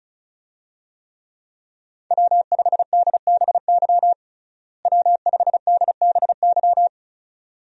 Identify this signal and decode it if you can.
Morse code, heard as a keyed tone, about 36 wpm, 700 Hz. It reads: W5DBY W5DBY
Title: USA Call Signs - 5 Characters - Repeated 2x - 35wpm